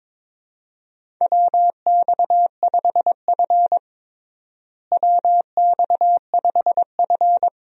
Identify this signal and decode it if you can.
Morse code, heard as a keyed tone, about 22 wpm, 700 Hz. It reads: WX5F WX5F
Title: USA Call Signs - 4 Characters - Repeated 3x - 22wpm